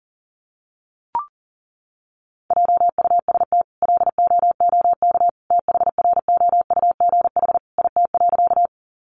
Morse code, heard as a keyed tone, about 40 wpm, 700 Hz. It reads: JUST LOOK THROUGH IT.
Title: Sentences from Top 200 Words 40wpm